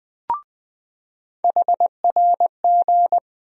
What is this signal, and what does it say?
Morse code, heard as a keyed tone, about 20 wpm, 700 Hz. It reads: HRG
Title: USA Call Sign Postfixes - 3 Letters 20wpm